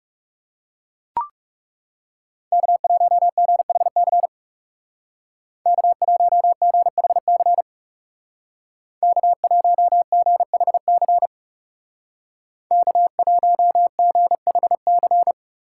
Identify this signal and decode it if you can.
Morse code, heard as a keyed tone, about 38 wpm, 700 Hz. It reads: K1GHC K1GHC K1GHC K1GHC